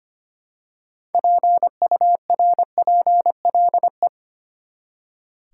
Morse code, heard as a keyed tone, about 25 wpm, 700 Hz. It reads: PURPLE
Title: Six Letter Words - 25wpm